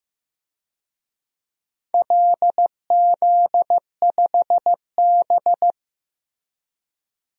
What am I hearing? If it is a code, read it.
Morse code, heard as a keyed tone, about 15 wpm, 700 Hz. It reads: LZ5B
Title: DXCC-100 Call Signs - 4 Characters - 15wpm